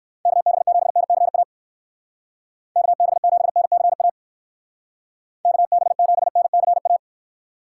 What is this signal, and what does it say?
Morse code, heard as a keyed tone, about 56 wpm, 700 Hz. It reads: KB7NCA KB7NCA KB7NCA